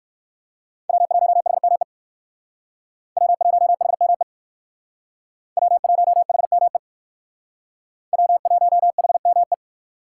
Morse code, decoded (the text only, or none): W1HME W1HME W1HME W1HME